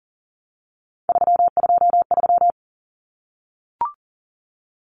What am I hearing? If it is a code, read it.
Morse code, heard as a keyed tone, about 40 wpm, 700 Hz. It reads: 323
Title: Common Cut Numbers - 3-Digit 40wpm